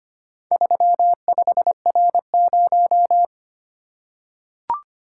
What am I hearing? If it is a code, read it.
Morse code, heard as a keyed tone, about 25 wpm, 700 Hz. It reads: 35R0
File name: practice / 25wpm / Four Letter-Number Combinations - 25wpm